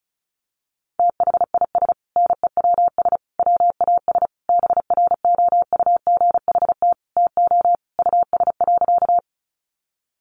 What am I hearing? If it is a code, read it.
Morse code, heard as a keyed tone, about 34 wpm, 700 Hz. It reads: THIS NEWS WAS BROUGHT TO US.
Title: Sentences from Top 800 Words 35wpm